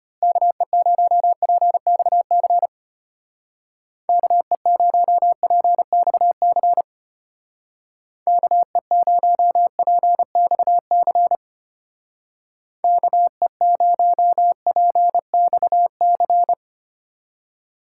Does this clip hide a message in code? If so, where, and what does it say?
Yes: Morse code from 0.2 to 16.5 s, heard as a keyed tone, about 32 wpm, 700 Hz. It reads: KE0PXC KE0PXC KE0PXC KE0PXC